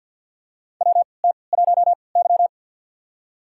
Morse code, heard as a keyed tone, about 52 wpm, spaced as for 29 wpm, 700 Hz. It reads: WT1Y
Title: USA Call Signs - 4 Characters - 3x Character Spacing - 50wpm